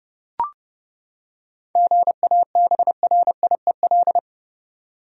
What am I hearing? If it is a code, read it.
Morse code, heard as a keyed tone, about 30 wpm, 700 Hz. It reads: GABRIEL
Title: USA Common Names - 30wpm